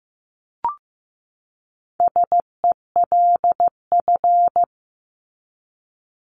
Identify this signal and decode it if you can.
Morse code, heard as a keyed tone, about 15 wpm, 700 Hz. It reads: SELF
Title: Top 300 Words 15wpm